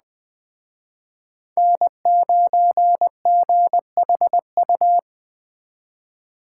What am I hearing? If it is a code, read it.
Morse code, heard as a keyed tone, about 20 wpm, 700 Hz. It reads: N9GHU